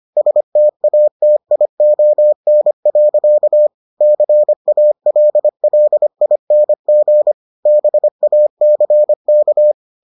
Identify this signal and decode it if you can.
Morse code, heard as a keyed tone, about 25 wpm, 600 Hz. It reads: STATION. CALLING BACK